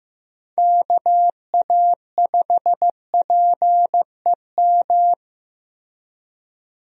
Morse code, heard as a keyed tone, about 15 wpm, 700 Hz. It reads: KA5PEM
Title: USA Call Signs - 6 Characters - Repeated 2x - 15wpm